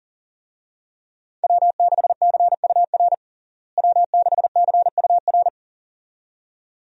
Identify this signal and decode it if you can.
Morse code, heard as a keyed tone, about 40 wpm, 700 Hz. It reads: W6CUR W6CUR
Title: USA Call Signs - 5 Characters - Repeated 2x - 40wpm